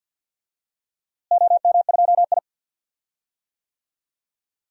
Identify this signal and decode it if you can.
Morse code, heard as a keyed tone, about 45 wpm, 700 Hz. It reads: OM2I
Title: DXCC-100 Call Signs - 4 Characters - Repeated 3x - 50wpm